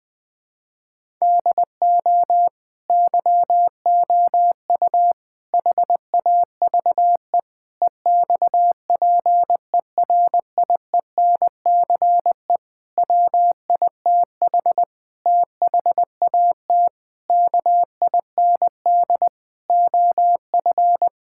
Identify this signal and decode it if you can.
Morse code, heard as a keyed tone, about 20 wpm, 700 Hz. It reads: DO YOU HAVE EXPERIENCE WITH THAT KIND OF
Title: Sentences from Top 1000 Words - Review Part 5 - 20wpm